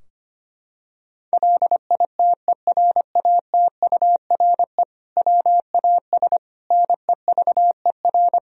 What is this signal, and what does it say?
Morse code, heard as a keyed tone, about 25 wpm, 700 Hz. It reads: LITERATURE WAS NEVER
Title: Sentences from Top 1000 Words - Review Part 3 - 25wpm